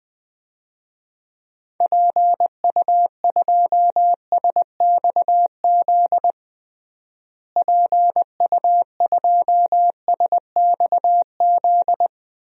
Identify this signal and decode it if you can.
Morse code, heard as a keyed tone, about 20 wpm, 700 Hz. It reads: PU2SXZ PU2SXZ
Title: DXCC-100 Call Signs - 6 Characters - Repeated 2x - 20wpm